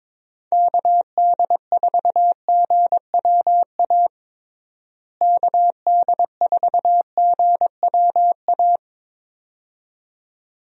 Morse code, heard as a keyed tone, about 22 wpm, 700 Hz. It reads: KD4GWA KD4GWA
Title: USA Call Signs - 6 Characters - Repeated 2x - 22wpm